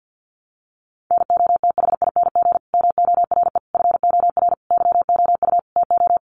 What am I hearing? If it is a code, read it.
Morse code, heard as a keyed tone, about 50 wpm, 700 Hz. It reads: NOTHING MORE FOR YOU TO
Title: Sentences from Top 1100 Words 50wpm